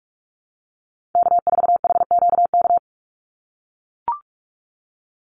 Morse code, heard as a keyed tone, about 44 wpm, 700 Hz. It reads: K4HQK
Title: Call Signs - 45wpm